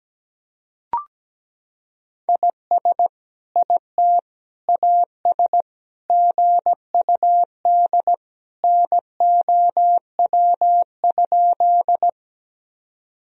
Morse code, heard as a keyed tone, about 17 wpm, 700 Hz. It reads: IS IT AS GUD NOW?
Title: Sentences from Top 200 Words and Abbreviations 17wpm